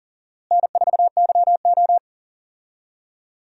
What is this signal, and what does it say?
Morse code, heard as a keyed tone, about 40 wpm, 700 Hz. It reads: N4YO
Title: USA Call Signs - 4 Characters - Repeated 3x - 40wpm